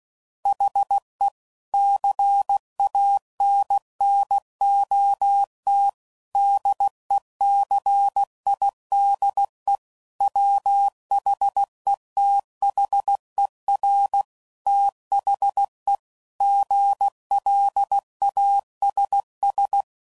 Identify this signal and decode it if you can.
Morse code, heard as a keyed tone, about 16 wpm, 800 Hz. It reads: HE CANNOT DECIDE WHETHER THE GLASS